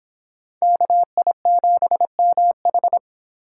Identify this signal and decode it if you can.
Morse code, heard as a keyed tone, about 26 wpm, 700 Hz. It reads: KI7MH